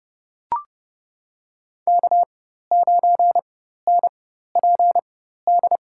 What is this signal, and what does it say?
Morse code, heard as a keyed tone, about 30 wpm, spaced as for 14 wpm, 700 Hz. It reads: K9NPD
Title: USA Call Signs - 4x Character Spacing - 30wpm